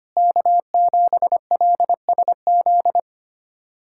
Morse code, heard as a keyed tone, about 25 wpm, 700 Hz. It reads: K7LSZ
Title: USA Callsign DE Callsign - 5 Characters - 25wpm